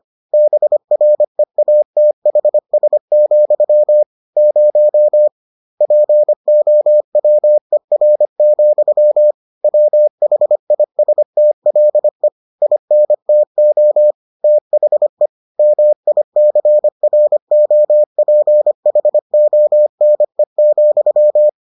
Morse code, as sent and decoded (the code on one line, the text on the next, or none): -... .-. .- - .... ... --..-- / ----- / .--. --- .-- . .-. --..-- / .-- .... .. ... - .-.. / .. -. - --- / - .... / -- .. -.-. .-. --- .--. .... --- -. --..--
BREATHS, 0 POWER, WHISTLE INTO THE MICROPHONE,